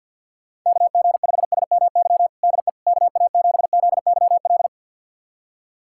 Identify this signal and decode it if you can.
Morse code, heard as a keyed tone, about 50 wpm, 700 Hz. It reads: KG5SMY DE KA7ZYL
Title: USA Callsign DE Callsign - 6 Characters - 50wpm